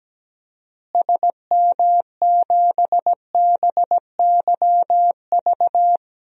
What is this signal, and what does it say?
Morse code, heard as a keyed tone, about 17 wpm, 700 Hz. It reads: SM7BYV
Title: DXCC-100 Call Signs - 6 Characters - Repeated 3x - 17wpm